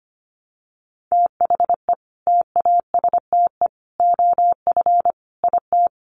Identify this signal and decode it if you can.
Morse code, heard as a keyed tone, about 25 wpm, 700 Hz. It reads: THE TASTE OF IT
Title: Sentences from Top 1100 Words 25wpm